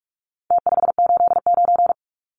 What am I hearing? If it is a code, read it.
Morse code, heard as a keyed tone, about 44 wpm, 700 Hz. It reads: T589
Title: Numbers - Contest Style - 4-Digit 45wpm